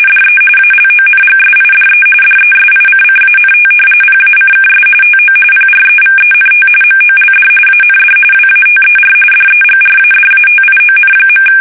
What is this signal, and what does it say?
RTTY, about 45.45 baud. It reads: DE ALONGSIDE DESICCANT PACKS TO PREVENT MOLD AND WATER DEGRADATION.
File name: fsk850 / battleshipnj